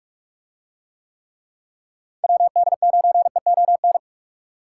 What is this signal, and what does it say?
Morse code, heard as a keyed tone, about 45 wpm, 700 Hz. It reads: WD9EON